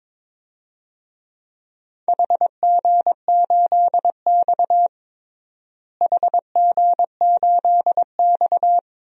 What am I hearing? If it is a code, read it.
Morse code, heard as a keyed tone, about 21 wpm, 700 Hz. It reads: HG8X HG8X
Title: DXCC-100 Call Signs - 4 Characters - Repeated 3x - 22wpm